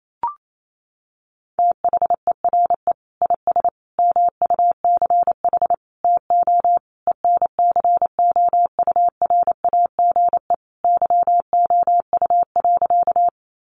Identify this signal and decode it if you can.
Morse code, heard as a keyed tone, about 28 wpm, 700 Hz. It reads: THERE IS MUCH TO ENCOURAGE YOU.
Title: Sentences from Top 1400 Words 28wpm